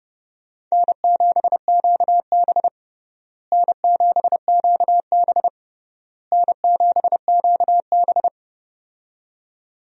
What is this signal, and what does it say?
Morse code, heard as a keyed tone, about 32 wpm, 700 Hz. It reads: N7QB N7QB N7QB
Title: USA Call Signs - 4 Characters - Repeated 3x - 30wpm